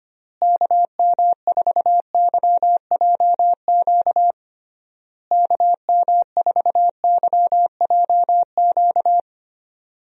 Morse code, heard as a keyed tone, about 25 wpm, 700 Hz. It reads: KM4YJQ KM4YJQ
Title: USA Call Signs - 6 Characters - Repeated 3x - 25wpm